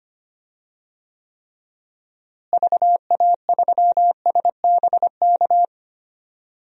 Morse code, heard as a keyed tone, about 25 wpm, 700 Hz. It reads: VA3SBK
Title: DXCC-100 Call Signs - 6 Characters - Repeated 2x - 25wpm